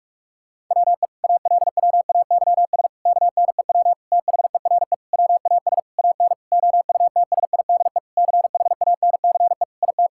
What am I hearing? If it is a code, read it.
Morse code, heard as a keyed tone, about 46 wpm, 700 Hz. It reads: WE ALWAYS KNEW THERE WAS AN OUTSIDE CHANCE IT
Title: Sentences from Top 1000 Words - Review Part 6 - 45wpm